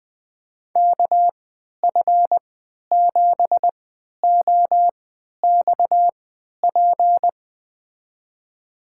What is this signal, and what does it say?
Morse code, heard as a keyed tone, about 20 wpm, spaced as for 11 wpm, 700 Hz. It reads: KF7OXP
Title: USA Call Signs - 6 Characters - 3x Character Spacing - 20wpm